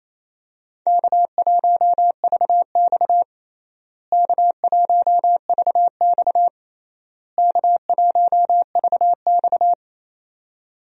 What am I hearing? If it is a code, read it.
Morse code, heard as a keyed tone, about 27 wpm, 700 Hz. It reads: K1VX K1VX K1VX
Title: USA Call Signs - 4 Characters - Repeated 3x - 28wpm